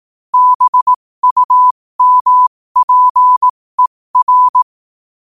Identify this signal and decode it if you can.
Morse code, heard as a keyed tone, about 18 wpm, spaced as for 15 wpm, 1000 Hz. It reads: BUMPER